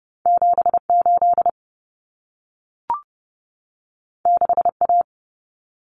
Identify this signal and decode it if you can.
Morse code, heard as a keyed tone, about 30 wpm, 700 Hz. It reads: 78 6A
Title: Common Cut Numbers - 2-Digit 30wpm